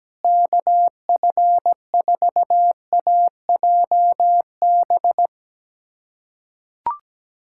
Morse code, heard as a keyed tone, about 17 wpm, 700 Hz. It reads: KF4AJB
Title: USA Callsign DE Callsign 17wpm